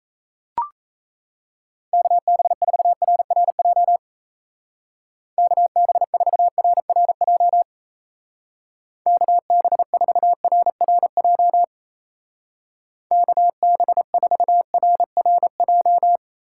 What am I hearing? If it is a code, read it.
Morse code, heard as a keyed tone, about 35 wpm, 700 Hz. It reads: KB4RRJ KB4RRJ KB4RRJ KB4RRJ